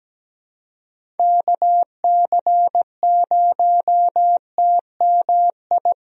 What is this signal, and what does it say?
Morse code, heard as a keyed tone, about 17 wpm, 700 Hz. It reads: KC0TMI
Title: USA Call Signs - 6 Characters - 17wpm